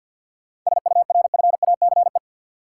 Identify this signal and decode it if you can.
Morse code, heard as a keyed tone, about 47 wpm, 700 Hz. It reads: SURFACE
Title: Sentences from Top 600 Words - New Word Drill - 50wpm